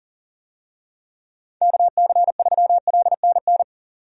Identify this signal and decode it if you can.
Morse code, heard as a keyed tone, about 37 wpm, 700 Hz. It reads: KC3LNN
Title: USA Call Signs - 6 Characters - 40wpm